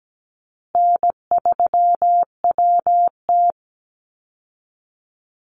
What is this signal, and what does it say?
Morse code, heard as a keyed tone, about 17 wpm, 700 Hz. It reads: N3WT
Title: Easy Call Signs Repeated 2x - 17wpm